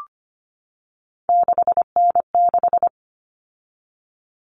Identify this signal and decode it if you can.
Morse code, heard as a keyed tone, about 25 wpm, 700 Hz. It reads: 6N6
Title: All Cut Numbers - 3-Digit 25wpm